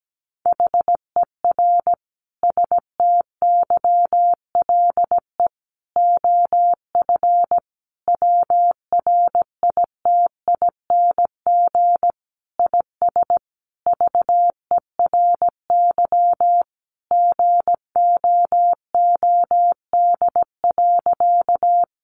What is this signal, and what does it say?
Morse code, heard as a keyed tone, about 17 wpm, 700 Hz. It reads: HER STYLE OF WRITING IS VERY GOOD.
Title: Sentences from Top 800 Words 17wpm